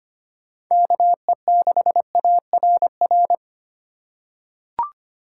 Morse code, heard as a keyed tone, about 25 wpm, 690 Hz. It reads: KE6ARR
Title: USA Call Signs - 6 Characters - 2x Character Spacing - 25wpm